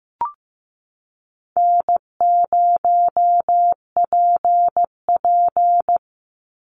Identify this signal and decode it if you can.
Morse code, heard as a keyed tone, about 15 wpm, 700 Hz. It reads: N0PP
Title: Easy Call Signs - 15wpm